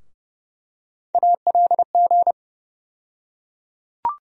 Morse code, heard as a keyed tone, about 30 wpm, 700 Hz. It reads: ALG
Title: USA Call Sign Postfixes - 3 Letters 30wpm